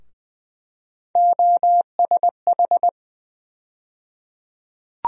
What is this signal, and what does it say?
Morse code, heard as a keyed tone, about 20 wpm, 700 Hz. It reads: OSH
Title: USA Call Sign Postfixes - 3 Letters 20wpm